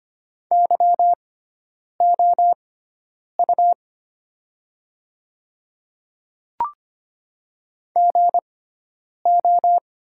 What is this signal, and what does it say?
Morse code, heard as a keyed tone, about 25 wpm, spaced as for 9 wpm, 690 Hz. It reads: YOU GO
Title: 6x Character Spacing - Top 500 Words - 25wpm